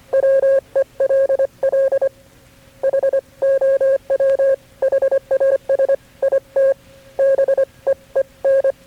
Morse code, heard as a keyed tone, about 24 wpm, 540 Hz. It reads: WELL HOW HAS IT BEEN
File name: KB6MT-tape3-side1